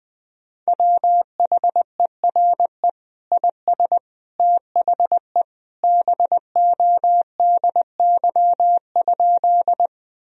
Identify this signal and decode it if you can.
Morse code, heard as a keyed tone, about 20 wpm, 700 Hz. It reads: WHERE IS THE BODY?
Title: Sentences from Top 1000 Words - Review Part 2 - 20wpm